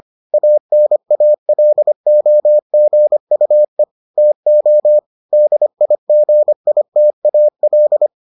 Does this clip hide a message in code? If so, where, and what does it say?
Yes: Morse code from 0.3 to 8.1 s, heard as a keyed tone, about 25 wpm, 600 Hz. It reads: ANALOGUE TO DIGITAL